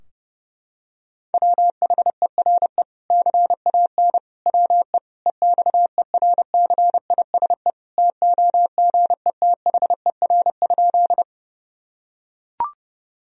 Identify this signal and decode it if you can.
Morse code, heard as a keyed tone, about 32 wpm, 700 Hz. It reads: WHERE CAN WE EXERCISE TOGETHER?
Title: Sentences from Top 1000 Words - Review Part 4 - 30wpm